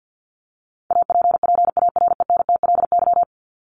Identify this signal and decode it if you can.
Morse code, heard as a keyed tone, about 50 wpm, 700 Hz. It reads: APPARENTLY
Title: LY Suffix - Top 100 Words with Base Words 50wpm